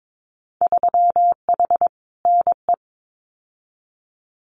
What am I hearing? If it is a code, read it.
Morse code, heard as a keyed tone, about 22 wpm, 700 Hz. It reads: 3H NE
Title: Winter Field Day Exchanges Encore 22wpm